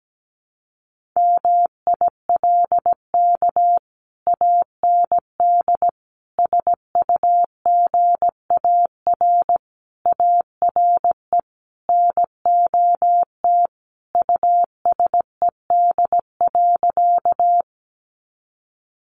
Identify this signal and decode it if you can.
Morse code, heard as a keyed tone, about 17 wpm, 700 Hz. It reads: MILK AND SUGAR ARE NOT USED.
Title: Sentences from Top 1500 Words 17wpm